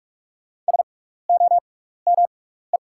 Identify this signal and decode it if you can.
Morse code, heard as a keyed tone, about 44 wpm, spaced as for 15 wpm, 690 Hz. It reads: SOME
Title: 6x Character Spacing - Top 100 Words - 45wpm